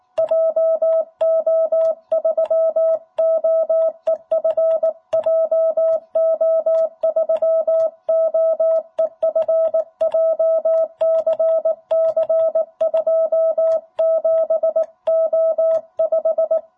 Morse code, heard as a keyed tone, about 18 wpm, 660 Hz. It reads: JO3OEF JO3OEF JCC 27O5
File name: jh8cbh / 2021cw